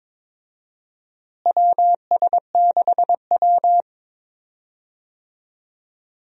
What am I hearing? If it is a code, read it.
Morse code, heard as a keyed tone, about 22 wpm, 700 Hz. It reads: WS6W
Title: USA Call Signs - 4 Characters - Repeated 3x - 22wpm